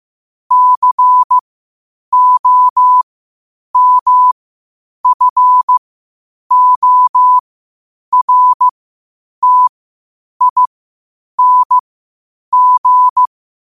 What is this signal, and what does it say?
Morse code, heard as a keyed tone, about 15 wpm, spaced as for 8 wpm, 1000 Hz. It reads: COMFORTING
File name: qotdfiles / 8wpm